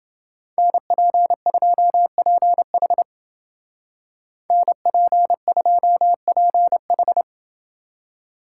Morse code, heard as a keyed tone, about 28 wpm, 700 Hz. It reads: NP2PH NP2PH